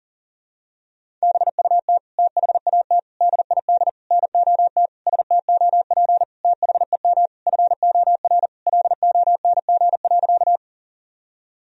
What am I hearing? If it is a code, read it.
Morse code, heard as a keyed tone, about 40 wpm, 700 Hz. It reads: BUT THAT DID NOT STOP THEM FOR LONG.
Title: Sentences from Top 1000 Words - Review Part 1 - 40wpm